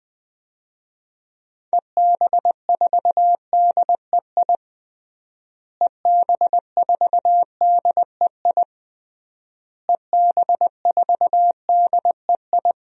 Morse code, heard as a keyed tone, about 21 wpm, 700 Hz. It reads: EB4DEI EB4DEI EB4DEI
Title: DXCC-100 Call Signs - 6 Characters - Repeated 3x - 20wpm